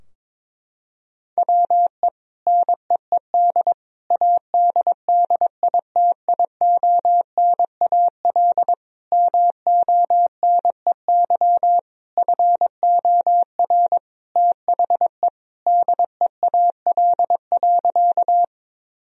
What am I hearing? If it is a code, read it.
Morse code, heard as a keyed tone, about 22 wpm, 700 Hz. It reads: WE NEED ADDITIONAL MONEY FOR THE DEAL.
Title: Sentences from Top 1000 Words - Review Part 4 - 22wpm